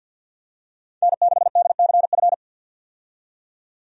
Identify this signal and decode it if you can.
Morse code, heard as a keyed tone, about 52 wpm, 700 Hz. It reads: N6DCF
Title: USA Call Signs - 5 Characters - 50wpm